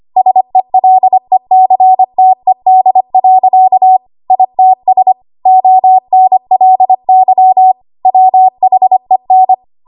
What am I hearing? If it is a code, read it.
Morse code, heard as a keyed tone, about 25 wpm, 750 Hz. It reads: SELECTED. ITS ONLY WHEN
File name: Archive / 25 WPM / 25WPM